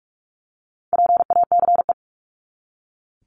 Morse code, heard as a keyed tone, about 44 wpm, 700 Hz. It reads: PACE
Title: Top 1751-2000 Words 45wpm